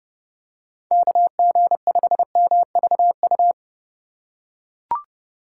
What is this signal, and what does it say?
Morse code, heard as a keyed tone, about 32 wpm, 700 Hz. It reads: KG5MVU
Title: USA Call Signs - 6 Characters - 6x Character Spacing - 30wpm